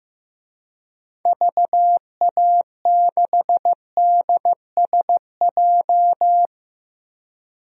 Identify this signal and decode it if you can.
Morse code, heard as a keyed tone, about 16 wpm, 700 Hz. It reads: VA6DSJ
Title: DXCC-100 Call Signs - 6 Characters - Repeated 2x - 15wpm